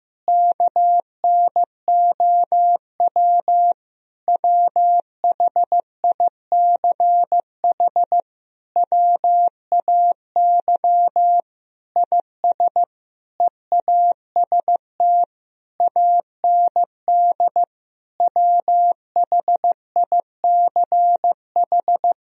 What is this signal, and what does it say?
Morse code, heard as a keyed tone, about 15 wpm, 700 Hz. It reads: KNOW WHICH WAY IS EAST AND WHICH
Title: Sentences from Top 1000 Words - Review Part 3 - 15wpm